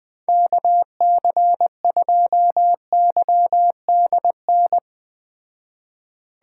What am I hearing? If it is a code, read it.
Morse code, heard as a keyed tone, about 20 wpm, 700 Hz. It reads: KC2YDN